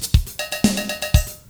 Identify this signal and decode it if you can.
There is no code.